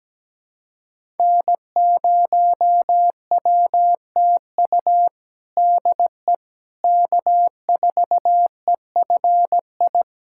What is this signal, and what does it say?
Morse code, heard as a keyed tone, about 17 wpm, 700 Hz. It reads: N0WTU DE K4EFI
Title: USA Callsign DE Callsign - 5 Characters - 17wpm